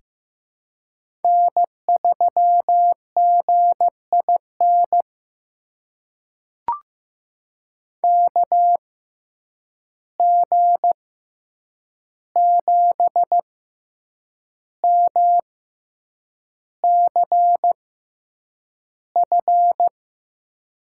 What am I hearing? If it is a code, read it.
Morse code, heard as a keyed tone, about 15 wpm, spaced as for 7 wpm, 700 Hz. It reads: N3GIN KG7MCF
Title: USA Call Signs - 6x Character Spacing - 15wpm